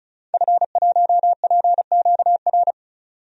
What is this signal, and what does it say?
Morse code, heard as a keyed tone, about 34 wpm, 700 Hz. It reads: F1PQR
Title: DXCC-100 Call Signs - 5 Characters - 35wpm